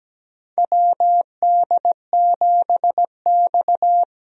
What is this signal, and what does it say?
Morse code, heard as a keyed tone, about 17 wpm, 700 Hz. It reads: WD7X
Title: USA Call Signs - 4 Characters - Repeated 2x - 17wpm